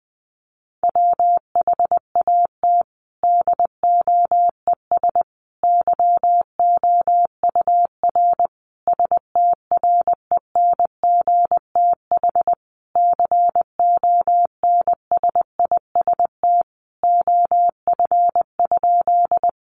Morse code, heard as a keyed tone, about 20 wpm, 700 Hz. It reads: WHAT DOES YOUR STRENGTH CONSIST OF?
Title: Sentences from Top 1400 Words 20wpm